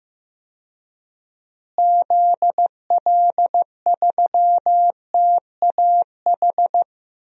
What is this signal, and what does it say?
Morse code, heard as a keyed tone, about 15 wpm, 700 Hz. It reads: ZL3TAH
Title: DXCC-100 Call Signs - 6 Characters - Repeated 3x - 15wpm